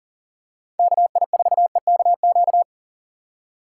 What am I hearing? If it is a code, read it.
Morse code, heard as a keyed tone, about 40 wpm, 700 Hz. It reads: KI4EKQ
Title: USA Call Signs - 6 Characters - 2x Character Spacing - 40wpm